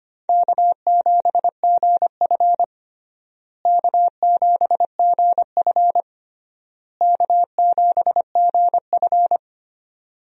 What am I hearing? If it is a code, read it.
Morse code, heard as a keyed tone, about 25 wpm, 700 Hz. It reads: K7GF K7GF K7GF
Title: USA Call Signs - 4 Characters - Repeated 3x - 25wpm